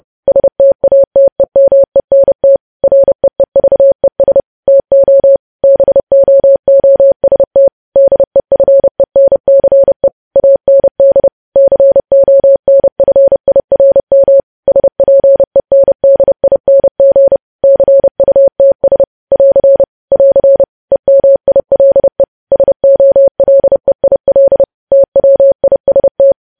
Morse code, heard as a keyed tone, about 30 wpm, 550 Hz. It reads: STATEMENT REEVES TO BOOST DEFENCE AND CONFIRM SPENDING CUTS <AR> <AR> EMILE SOLEIL TWIST
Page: news headlines in morse code at 30 wpm